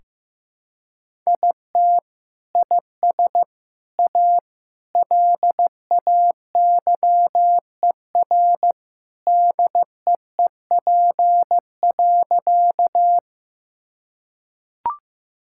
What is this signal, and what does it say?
Morse code, heard as a keyed tone, about 16 wpm, 700 Hz. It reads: IT IS A LAYER DEEP.
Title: Sentences from Top 1600 Words 15wpm